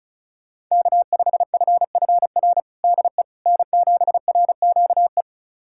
Morse code, heard as a keyed tone, about 36 wpm, 700 Hz. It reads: K5FFR DE N7RQE
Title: USA Callsign DE Callsign - 5 Characters - 35wpm